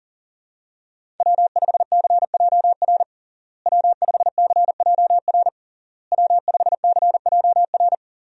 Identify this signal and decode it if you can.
Morse code, heard as a keyed tone, about 40 wpm, 700 Hz. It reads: W5CJR W5CJR W5CJR
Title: USA Call Signs - 5 Characters - Repeated 3x - 40wpm